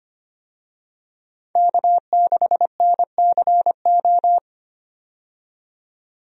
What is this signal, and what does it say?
Morse code, heard as a keyed tone, about 25 wpm, 700 Hz. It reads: K6NCO
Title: USA Call Signs - 5 Characters - 25wpm